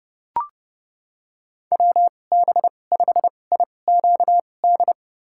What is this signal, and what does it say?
Morse code, heard as a keyed tone, about 32 wpm, spaced as for 22 wpm, 700 Hz. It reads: WB5IQD
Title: USA Call Signs - 6 Characters - 2x Character Spacing - 30wpm